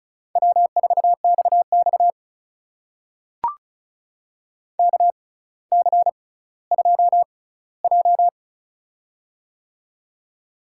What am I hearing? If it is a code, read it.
Morse code, heard as a keyed tone, about 36 wpm, spaced as for 18 wpm, 700 Hz. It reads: W4XX KC2J
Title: USA Call Signs - 4 Characters - 6x Character Spacing - 35wpm